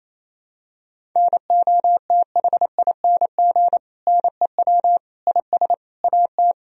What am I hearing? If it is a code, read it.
Morse code, heard as a keyed tone, about 28 wpm, 700 Hz. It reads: NOTHING NEW IS AT
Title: Sentences from Top 1000 Words - Review Part 1 - 28wpm